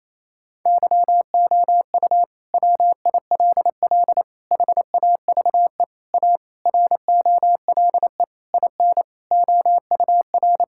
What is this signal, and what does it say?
Morse code, heard as a keyed tone, about 27 wpm, 700 Hz. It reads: YOU WILL HAVE A ROLE IN OUR
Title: Sentences from Top 1000 Words - Review Part 6 - 28wpm